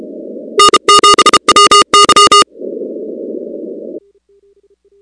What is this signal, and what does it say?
Morse code, heard as a keyed tone, about 33 wpm, 1220 Hz, with noise. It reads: N7WY
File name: contests / 2019-cw-wpx-kt5j-k5pi- -2330 0-19